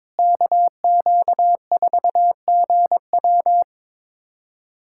Morse code, heard as a keyed tone, about 22 wpm, 700 Hz. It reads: KQ4GW
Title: USA Callsign DE Callsign - 5 Characters - 22wpm